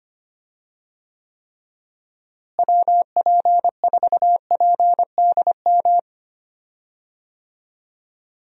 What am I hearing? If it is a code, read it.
Morse code, heard as a keyed tone, about 25 wpm, 700 Hz. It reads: WP4PDM
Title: USA Call Signs - 6 Characters - Repeated 3x - 25wpm